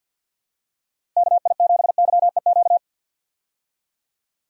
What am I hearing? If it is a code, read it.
Morse code, heard as a keyed tone, about 52 wpm, 700 Hz. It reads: KI7YEQ
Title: USA Call Signs - 6 Characters - Repeated 3x - 50wpm